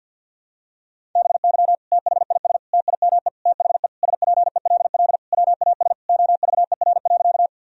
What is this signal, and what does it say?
Morse code, heard as a keyed tone, about 50 wpm, 700 Hz. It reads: BY THIS TIME THE SPELL WAS OVER.
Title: Sentences from Top 1000 Words - Review Part 1 - 50wpm